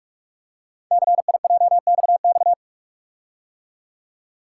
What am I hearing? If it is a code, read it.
Morse code, heard as a keyed tone, about 44 wpm, 700 Hz. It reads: CIJXX
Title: Code Groups - 5 Letters - 45wpm